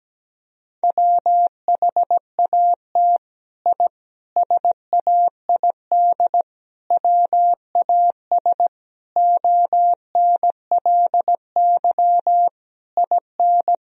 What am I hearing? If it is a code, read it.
Morse code, heard as a keyed tone, about 17 wpm, 700 Hz. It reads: WHAT I SAID WAS ONLY IN
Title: Sentences from Top 1000 Words - Review Part 4 - 17wpm